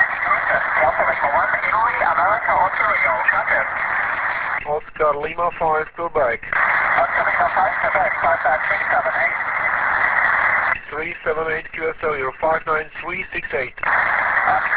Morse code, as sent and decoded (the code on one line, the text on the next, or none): none